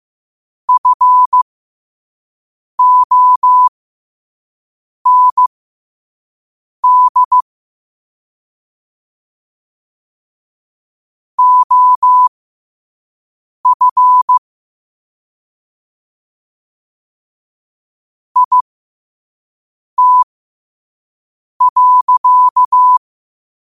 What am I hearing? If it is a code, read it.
Morse code, heard as a keyed tone, about 15 wpm, spaced as for 5 wpm, 1000 Hz. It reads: FOND OF IT.
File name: qotdfiles / 5wpm